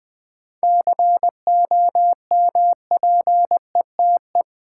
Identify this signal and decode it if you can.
Morse code, heard as a keyed tone, about 20 wpm, 700 Hz. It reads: COMPETE
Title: 7 Letter Words - First 3 Letters Given - 20wpm